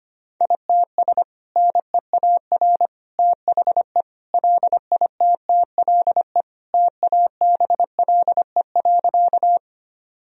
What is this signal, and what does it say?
Morse code, heard as a keyed tone, about 25 wpm, 700 Hz. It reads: ITS NEAR THE LITTLE TABLE.
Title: Sentences from Top 1000 Words - Review Part 2 - 25wpm